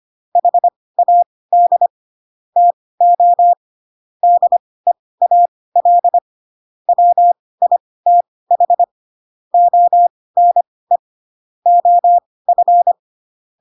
Morse code, heard as a keyed tone, about 25 wpm, spaced as for 18 wpm, 700 Hz. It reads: HAD TO DEAL WITH ONE OF
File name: Pinocchio / Pinocchio ch6 12 25 ik1hge